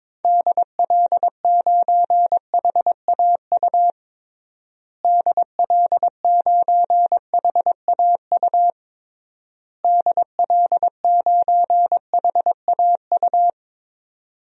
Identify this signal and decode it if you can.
Morse code, heard as a keyed tone, about 22 wpm, 700 Hz. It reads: DL9HAU DL9HAU DL9HAU
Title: DXCC-100 Call Signs - 6 Characters - Repeated 3x - 22wpm